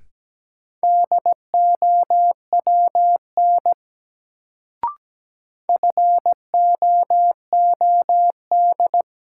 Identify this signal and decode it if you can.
Morse code, heard as a keyed tone, about 17 wpm, 700 Hz. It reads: DOWN FOOD
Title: Sentences from Top 300 Words - New Word Drill - 17wpm